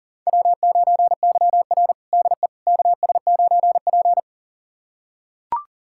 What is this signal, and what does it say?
Morse code, heard as a keyed tone, about 40 wpm, 700 Hz. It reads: W9YR DE KS9P
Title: USA Callsign DE Callsign - 4 Characters - 40wpm